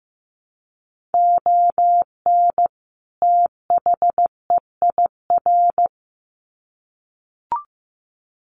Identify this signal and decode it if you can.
Morse code, heard as a keyed tone, about 15 wpm, 700 Hz. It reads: ON THEIR
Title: Sets of 2 Words 15wpm